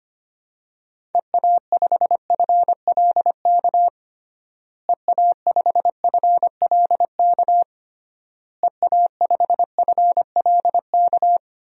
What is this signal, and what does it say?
Morse code, heard as a keyed tone, about 25 wpm, 700 Hz. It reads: EA5FLK EA5FLK EA5FLK
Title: DXCC-100 Call Signs - 6 Characters - Repeated 3x - 25wpm